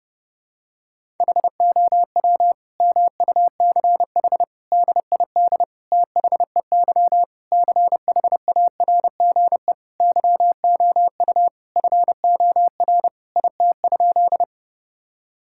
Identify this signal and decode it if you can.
Morse code, heard as a keyed tone, about 31 wpm, 700 Hz. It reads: HOW MUCH DID THEY CHARGE YOU FOR IT?
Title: Sentences from Top 1000 Words - Review Part 6 - 30wpm